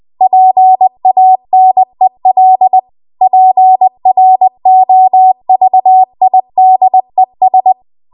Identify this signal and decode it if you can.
Morse code, heard as a keyed tone, about 20 wpm, 750 Hz. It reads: PANEL PROVIDES
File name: Archive / 20 WPM / 20WPM